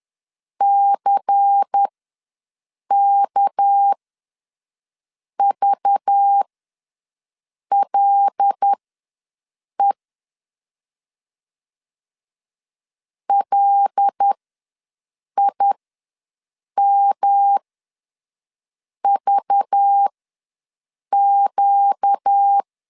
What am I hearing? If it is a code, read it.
Morse code, heard as a keyed tone, about 10 wpm, spaced as for 5 wpm, 800 Hz. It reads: CKVLE LIMVQ
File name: LEZIONI / 6 PKX0 / 6 PK ws 10wpm 3745